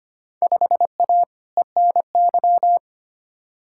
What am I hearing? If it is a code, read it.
Morse code, heard as a keyed tone, about 25 wpm, 700 Hz. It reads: 5A ENY
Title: Field Day Exchanges Encore - 25wpm